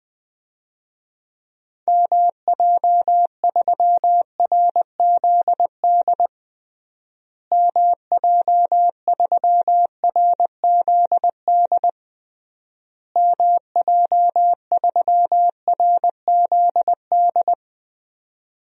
Morse code, heard as a keyed tone, about 21 wpm, 700 Hz. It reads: MJ3RZD MJ3RZD MJ3RZD
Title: DXCC-100 Call Signs - 6 Characters - Repeated 3x - 20wpm